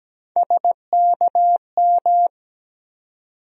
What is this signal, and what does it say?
Morse code, heard as a keyed tone, about 17 wpm, 700 Hz. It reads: SKM
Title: USA Call Sign Postfixes - 3 Letters 17wpm